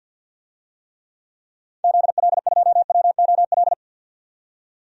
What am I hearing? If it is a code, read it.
Morse code, heard as a keyed tone, about 48 wpm, 700 Hz. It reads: ZL2WOL
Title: DXCC-100 Call Signs - 6 Characters - 50wpm